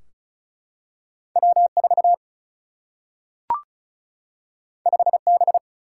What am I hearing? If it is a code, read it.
Morse code, heard as a keyed tone, about 36 wpm, 700 Hz. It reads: W4 5B
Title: Single-Letter Single-Number 35wpm